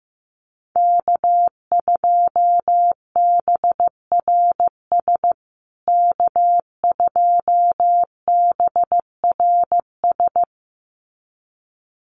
Morse code, heard as a keyed tone, about 15 wpm, 700 Hz. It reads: K2BRS K2BRS
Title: USA Call Signs Repeated 2x 15wpm